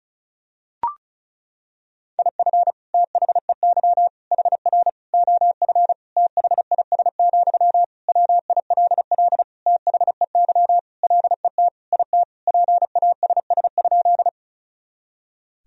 Morse code, heard as a keyed tone, about 36 wpm, 700 Hz. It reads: IF THEY HR OF THIS, WILL THEY LET IT PASS?
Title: Sentences from Top 400 Words and Abbreviations 35wpm